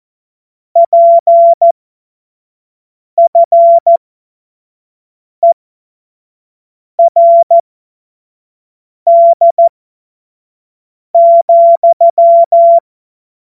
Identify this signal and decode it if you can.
Morse code, heard as a keyed tone, about 14 wpm, spaced as for 5 wpm, 680 Hz. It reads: PFERD,
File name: 25-F70